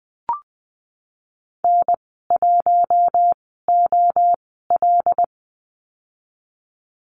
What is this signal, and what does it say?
Morse code, heard as a keyed tone, about 20 wpm, spaced as for 15 wpm, 700 Hz. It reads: N1OL
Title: 122 Lesson 20g - Number 3 - Call Signs 20wpm